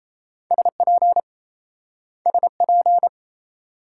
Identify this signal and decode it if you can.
Morse code, heard as a keyed tone, about 36 wpm, 700 Hz. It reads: SP SP